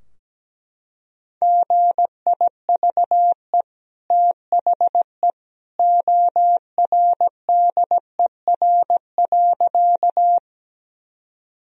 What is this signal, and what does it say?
Morse code, heard as a keyed tone, about 17 wpm, 700 Hz. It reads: GIVE THE ORDER.
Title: Sentences from Top 1000 Words - Review Part 2 - 17wpm